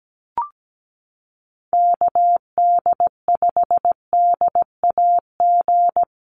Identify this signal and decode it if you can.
Morse code, heard as a keyed tone, about 17 wpm, 700 Hz. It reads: KD5DAG
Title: USA Call Signs Repeated 2x 17wpm